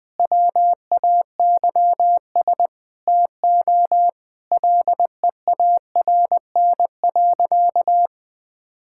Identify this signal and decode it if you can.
Morse code, heard as a keyed tone, about 20 wpm, 700 Hz. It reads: WAYS TO LEARN.
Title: Sentences with 6-Word Phrases - 20wpm